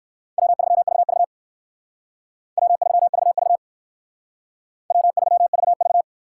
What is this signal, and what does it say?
Morse code, heard as a keyed tone, about 58 wpm, 700 Hz. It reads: W3VV W3VV W3VV